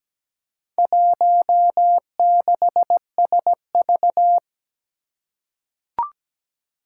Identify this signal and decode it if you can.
Morse code, heard as a keyed tone, about 17 wpm, 700 Hz. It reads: 16SV
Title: Four Challenging Letter-Number Combinations 17wpm